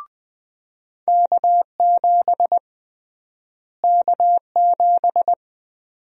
Morse code, heard as a keyed tone, about 21 wpm, 700 Hz. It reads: K7 K7
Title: USA Call Sign Prefixes - Repeated 2x - 20wpm